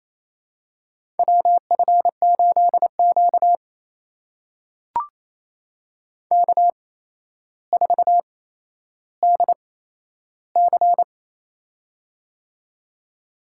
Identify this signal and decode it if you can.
Morse code, heard as a keyed tone, about 28 wpm, spaced as for 12 wpm, 700 Hz. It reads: WF8Q K4DC
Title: USA Call Signs - 4 Characters - 8x Character Spacing - 28wpm